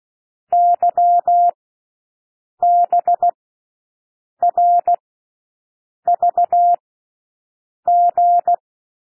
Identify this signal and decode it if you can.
Morse code, heard as a keyed tone, about 16 wpm, spaced as for 6 wpm, 700 Hz. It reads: YBRVG
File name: cwmp3 / L1 SOUND LETTERS 16 wpm